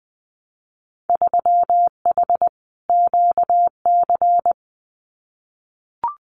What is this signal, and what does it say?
Morse code, heard as a keyed tone, about 20 wpm, 700 Hz. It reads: 3H QC
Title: Winter Field Day Exchanges 20wpm